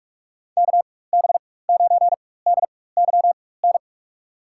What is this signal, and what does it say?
Morse code, heard as a keyed tone, about 45 wpm, spaced as for 21 wpm, 690 Hz. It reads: KB9DYN
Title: USA Call Signs - 4x Character Spacing - 45wpm